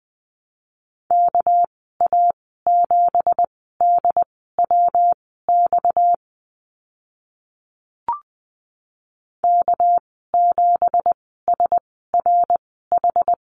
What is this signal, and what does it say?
Morse code, heard as a keyed tone, about 20 wpm, spaced as for 14 wpm, 700 Hz. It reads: KA7DWX K7SRH
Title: 231 Lesson 37g - Letter x - Call Signs 20wpm